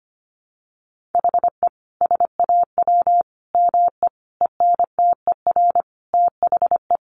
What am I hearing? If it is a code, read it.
Morse code, heard as a keyed tone, about 25 wpm, 700 Hz. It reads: HE SAW ME ENTER THE
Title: Sentences from Top 1300 Words 25wpm